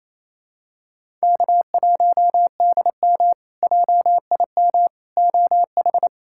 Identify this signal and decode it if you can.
Morse code, heard as a keyed tone, about 27 wpm, 700 Hz. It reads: K1DM JIM OH
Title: K1USN SST QSO Practice - 28wpm